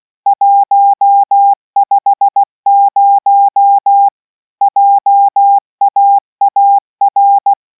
Morse code, heard as a keyed tone, about 16 wpm, 800 Hz. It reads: 150 JAAR